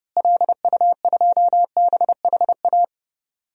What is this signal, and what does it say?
Morse code, heard as a keyed tone, about 30 wpm, 700 Hz. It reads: LU2BHA
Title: DXCC-100 Call Signs - 6 Characters - Repeated 2x - 30wpm